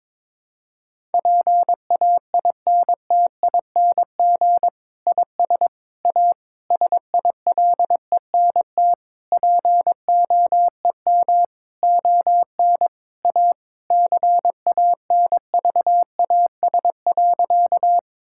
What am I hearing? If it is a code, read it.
Morse code, heard as a keyed tone, about 21 wpm, 700 Hz. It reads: PAINTING IS A SILENT POEM ON A CANVAS.
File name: Difficult Sentences with Hints - 22wpm